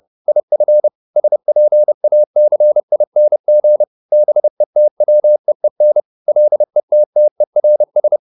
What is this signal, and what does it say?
Morse code, heard as a keyed tone, about 30 wpm, 600 Hz. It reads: IF SPACING BETWEEN LETTERS